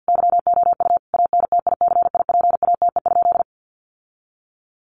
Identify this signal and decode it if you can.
Morse code, heard as a keyed tone, about 50 wpm, 700 Hz. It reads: YOU ANTICIPATE?
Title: Sentences from Top 1500 Words 50wpm